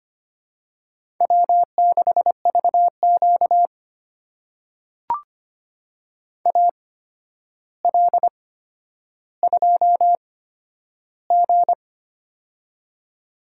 Morse code, heard as a keyed tone, about 25 wpm, spaced as for 11 wpm, 700 Hz. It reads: W6VQ AL2G
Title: USA Call Signs - 4 Characters - 8x Character Spacing - 25wpm